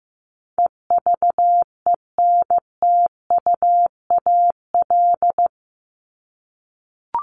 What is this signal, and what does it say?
Morse code, heard as a keyed tone, about 14 wpm, 700 Hz. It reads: EVENTUAL
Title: LY Suffix - Top 100 Words with Base Words 15wpm